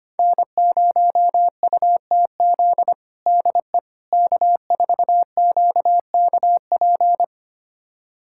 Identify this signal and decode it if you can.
Morse code, heard as a keyed tone, about 25 wpm, 700 Hz. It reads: N0UTZ DE K4QKP
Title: USA Callsign DE Callsign - 5 Characters - 25wpm